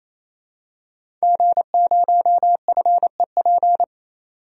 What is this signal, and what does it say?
Morse code, heard as a keyed tone, about 28 wpm, 700 Hz. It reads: G0FEP
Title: DXCC-100 Call Signs - 5 Characters - 28wpm